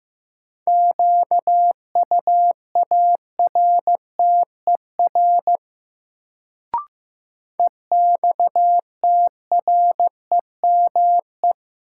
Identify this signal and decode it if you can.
Morse code, heard as a keyed tone, about 15 wpm, 700 Hz. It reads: QUARTER EXTREME
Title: Sentences from Top 1300 Words - New Word Drill - 15wpm